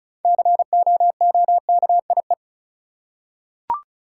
Morse code, heard as a keyed tone, about 36 wpm, 700 Hz. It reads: COOKIE
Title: Six Letter Words - 35wpm